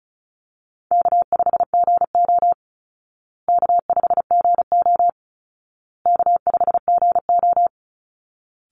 Morse code, heard as a keyed tone, about 34 wpm, 700 Hz. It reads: K5GO K5GO K5GO
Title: Call Signs Repeated 2x - 35wpm